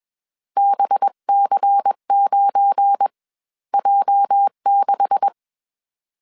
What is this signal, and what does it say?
Morse code, heard as a keyed tone, about 21 wpm, spaced as for 16 wpm, 800 Hz. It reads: BC9J6